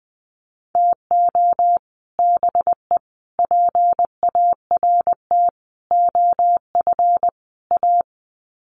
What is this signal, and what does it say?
Morse code, heard as a keyed tone, about 20 wpm, 700 Hz. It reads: TO BE PART OF A
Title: Sets of 5 Words Encore2 20wpm